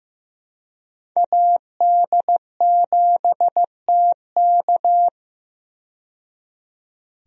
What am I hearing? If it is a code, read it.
Morse code, heard as a keyed tone, about 15 wpm, 700 Hz. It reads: AD7TK
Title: USA Call Signs - 5 Characters - 15wpm